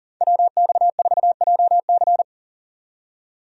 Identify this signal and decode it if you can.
Morse code, heard as a keyed tone, about 39 wpm, 700 Hz. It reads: WX4JC
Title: USA Call Signs - 5 Characters - 40wpm